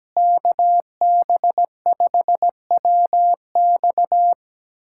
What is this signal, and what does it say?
Morse code, heard as a keyed tone, about 17 wpm, 700 Hz. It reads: KB5WX
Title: USA Callsign DE Callsign - 5 Characters - 17wpm